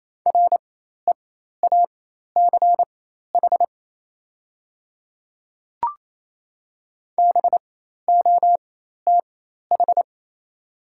Morse code, heard as a keyed tone, about 28 wpm, spaced as for 13 wpm, 700 Hz. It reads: REACH BOTH
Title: 4x Character Spacing - Top 500 Words - 28wpm